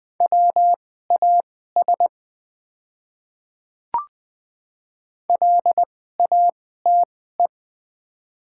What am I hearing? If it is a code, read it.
Morse code, heard as a keyed tone, about 20 wpm, spaced as for 14 wpm, 700 Hz. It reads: WAS LATE
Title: 2x Character Spacing - Top 300 Words - 20wpm